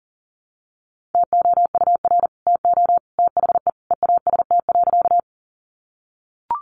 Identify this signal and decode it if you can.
Morse code, heard as a keyed tone, about 40 wpm, 700 Hz. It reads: TOUR TO THE EAST.
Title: Sentences from Top 1400 Words 40wpm